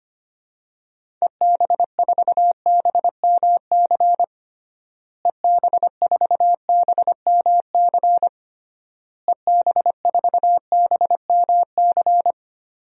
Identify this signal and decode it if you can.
Morse code, heard as a keyed tone, about 25 wpm, 700 Hz. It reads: EB4BMC EB4BMC EB4BMC
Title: DXCC-100 Call Signs - 6 Characters - Repeated 3x - 25wpm